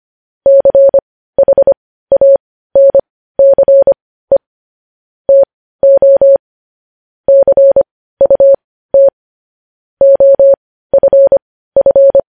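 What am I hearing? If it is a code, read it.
Morse code, heard as a keyed tone, about 25 wpm, spaced as for 15 wpm, 550 Hz. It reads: CHANCE TO CUT OFF
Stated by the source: Morse code news headlines sent at 15 wpm.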